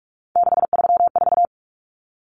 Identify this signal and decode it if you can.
Morse code, heard as a keyed tone, about 44 wpm, 700 Hz. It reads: 634
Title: Common Cut Numbers - 3-Digit 45wpm